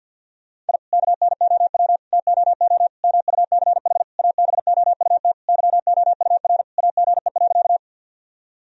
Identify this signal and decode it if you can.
Morse code, heard as a keyed tone, about 46 wpm, 700 Hz. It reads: I KNOW TOO MUCH ABOUT YOUR AGE.
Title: Sentences from Top 1000 Words - Review Part 3 - 50wpm